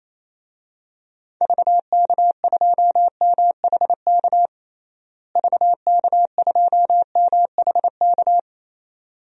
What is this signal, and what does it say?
Morse code, heard as a keyed tone, about 28 wpm, 700 Hz. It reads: VK2MHK VK2MHK
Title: DXCC-100 Call Signs - 6 Characters - Repeated 2x - 28wpm